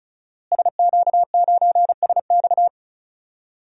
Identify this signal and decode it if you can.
Morse code, heard as a keyed tone, about 36 wpm, 700 Hz. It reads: SQ9SX
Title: DXCC-100 Call Signs - 5 Characters - Repeated 3x - 35wpm